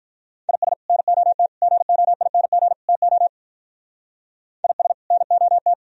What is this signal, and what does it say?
Morse code, heard as a keyed tone, about 42 wpm, 700 Hz. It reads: IS NOT GOING TO IS NOT
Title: Speed-Racing - Sets of 4 Words 35wpm